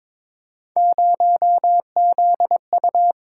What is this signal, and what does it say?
Morse code, heard as a keyed tone, about 22 wpm, 700 Hz. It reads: 0ZU
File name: practice / 22wpm / Three Letter-Number Combinations - 22wpm